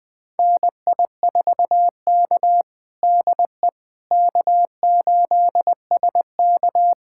Morse code, heard as a keyed tone, about 20 wpm, 700 Hz. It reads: NI4K DE K8SK
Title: USA Callsign DE Callsign - 4 Characters - 20wpm